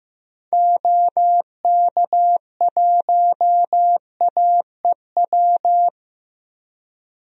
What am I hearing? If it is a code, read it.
Morse code, heard as a keyed tone, about 15 wpm, 700 Hz. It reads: OK1AEW
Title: DXCC-100 Call Signs - 6 Characters - 15wpm